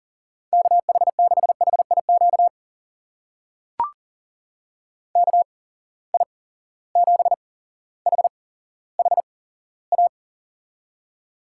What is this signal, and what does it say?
Morse code, heard as a keyed tone, about 40 wpm, spaced as for 17 wpm, 700 Hz. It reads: KH6HIQ KI7HHA
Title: USA Call Signs - 6 Characters - 8x Character Spacing - 40wpm